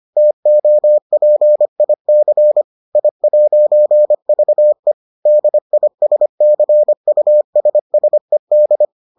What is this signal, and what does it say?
Morse code, heard as a keyed tone, about 25 wpm, 600 Hz. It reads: TOPIC I'VE DISCUSSED